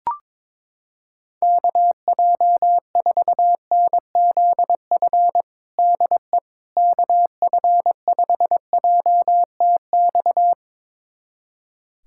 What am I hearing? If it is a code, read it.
Morse code, heard as a keyed tone, about 22 wpm, 700 Hz. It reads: KJ4NZF DE KF5JTX
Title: USA Callsign DE Callsign - 6 Characters - 22wpm